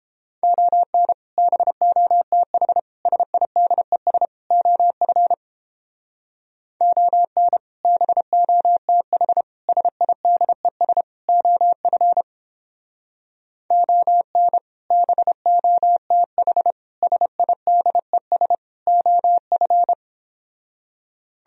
Morse code, heard as a keyed tone, about 29 wpm, 700 Hz. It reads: ON BOTH SIDES OF ON BOTH SIDES OF ON BOTH SIDES OF